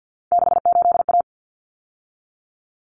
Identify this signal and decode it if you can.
Morse code, heard as a keyed tone, about 50 wpm, 700 Hz. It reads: 68A
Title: Common Cut Numbers - 3-Digit 50wpm